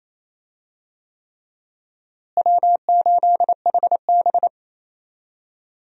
Morse code, heard as a keyed tone, about 27 wpm, 700 Hz. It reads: W8HB
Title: USA Call Signs - 4 Characters - Repeated 3x - 28wpm